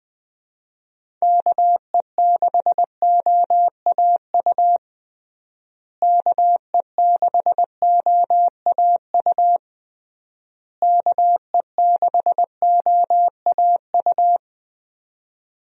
Morse code, heard as a keyed tone, about 21 wpm, 700 Hz. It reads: KE6OAU KE6OAU KE6OAU
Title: USA Call Signs - 6 Characters - Repeated 3x - 20wpm